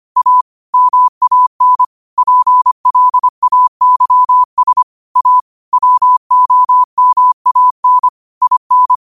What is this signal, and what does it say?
Morse code, heard as a keyed tone, about 25 wpm, 1000 Hz. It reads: A MAN PLAYS A WOMAN IN
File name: qotdfiles / 25wpm